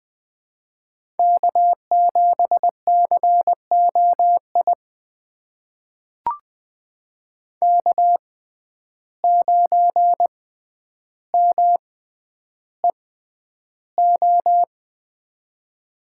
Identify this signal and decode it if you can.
Morse code, heard as a keyed tone, about 20 wpm, spaced as for 10 wpm, 700 Hz. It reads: K7COI K9MEO
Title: USA Call Signs - 5 Characters - 6x Character Spacing - 20wpm